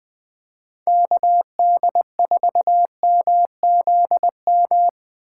Morse code, heard as a keyed tone, about 20 wpm, 700 Hz. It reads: KD4MZM
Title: 2023 POTA Activator Call Signs - 20wpm